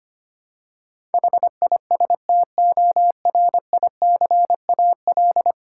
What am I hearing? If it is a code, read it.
Morse code, heard as a keyed tone, about 25 wpm, 700 Hz. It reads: HISTORICAL
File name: Ten Letter Words - 25wpm